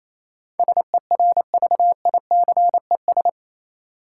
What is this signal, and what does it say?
Morse code, heard as a keyed tone, about 28 wpm, 700 Hz. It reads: SERVICES
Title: Sets of 5 Words - New Word Drill - 28wpm